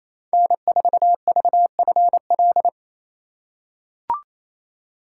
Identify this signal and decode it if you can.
Morse code, heard as a keyed tone, about 28 wpm, 700 Hz. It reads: N4VFL
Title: USA Call Signs - 3x Character Spacing - 28wpm